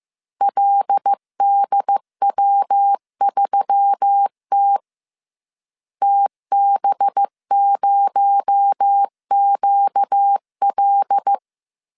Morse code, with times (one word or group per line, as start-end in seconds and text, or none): 0.4-4.8 LDW3T
6.0-11.3 TB0QL